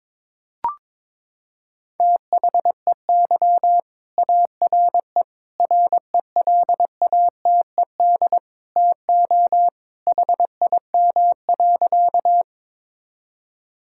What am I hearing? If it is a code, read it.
Morse code, heard as a keyed tone, about 22 wpm, 700 Hz. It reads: THEY ARE RELATED TO HIM.
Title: Sentences from Top 1000 Words - Review Part 5 - 22wpm